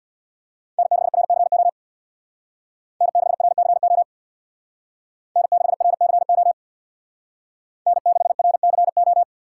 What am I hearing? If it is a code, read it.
Morse code, heard as a keyed tone, about 61 wpm, 700 Hz. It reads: N6RCO N6RCO N6RCO N6RCO
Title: Speed Racing - USA Call Signs 50wpm